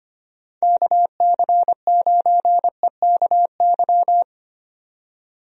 Morse code, heard as a keyed tone, about 25 wpm, 700 Hz. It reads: KC9EKY
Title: USA Call Signs - 6 Characters - 25wpm